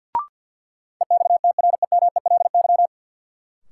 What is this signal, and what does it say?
Morse code, heard as a keyed tone, about 53 wpm, 700 Hz. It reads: EXTREMELY
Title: Sentences from Top 1000 Words - Review Part 7 - New Word Drill - 50wpm